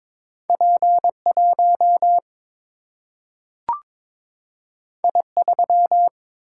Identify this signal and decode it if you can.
Morse code, heard as a keyed tone, about 22 wpm, 700 Hz. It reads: P1 I3
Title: Single-Letter Single-Number 22wpm